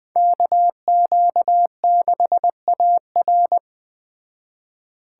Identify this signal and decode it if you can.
Morse code, heard as a keyed tone, about 20 wpm, 700 Hz. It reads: KQ6AR